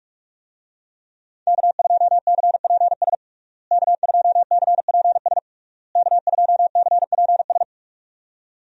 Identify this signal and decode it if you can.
Morse code, heard as a keyed tone, about 41 wpm, 700 Hz. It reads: K2CPS K2CPS K2CPS
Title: USA Call Signs - 5 Characters - Repeated 3x - 45wpm